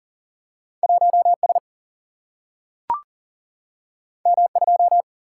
Morse code, heard as a keyed tone, about 40 wpm, 700 Hz. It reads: 1S M2
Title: Single-Letter Single-Number 40wpm